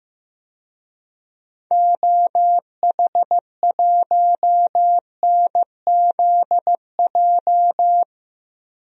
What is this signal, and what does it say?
Morse code, heard as a keyed tone, about 15 wpm, 700 Hz. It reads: OH1NZJ
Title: DXCC-100 Call Signs - 6 Characters - 15wpm